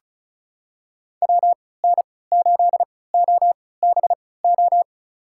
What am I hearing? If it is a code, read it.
Morse code, heard as a keyed tone, about 36 wpm, spaced as for 20 wpm, 700 Hz. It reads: WN8OBO
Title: USA Call Signs - 6 Characters - 3x Character Spacing - 35wpm